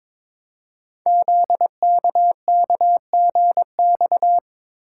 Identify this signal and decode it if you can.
Morse code, heard as a keyed tone, about 22 wpm, 700 Hz. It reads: ZKKGX
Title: Code Groups - 5 Letter:Numbers - 22wpm